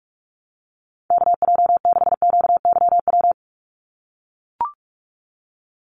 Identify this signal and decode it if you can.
Morse code, heard as a keyed tone, about 44 wpm, 700 Hz. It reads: KJ6QYW
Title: USA Call Signs 45wpm